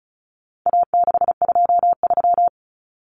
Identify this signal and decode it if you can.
Morse code, heard as a keyed tone, about 34 wpm, 700 Hz. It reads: A623
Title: Common Cut Numbers - 4-Digit 35wpm